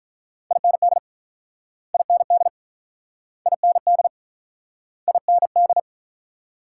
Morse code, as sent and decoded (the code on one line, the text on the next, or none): .. -. -.. / .. -. -.. / .. -. -.. / .. -. -..
IND IND IND IND